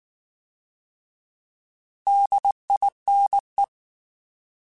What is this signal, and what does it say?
Morse code, heard as a keyed tone, about 19 wpm, 800 Hz. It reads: DINE
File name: Alphabetical / Alphabetical mp3s / Ds @18